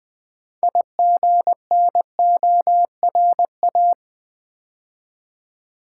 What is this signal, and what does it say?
Morse code, heard as a keyed tone, about 20 wpm, 700 Hz. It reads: IGNORA
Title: Build A Word - 3 Letters At A Time - 9-Letter Words - 20wpm